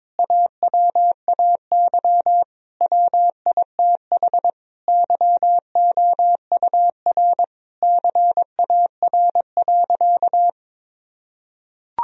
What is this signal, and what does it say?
Morse code, heard as a keyed tone, about 22 wpm, 700 Hz. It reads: AWAY WITH YOUR CAR.
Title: Sentences from Top 1000 Words - Review Part 1 - 22wpm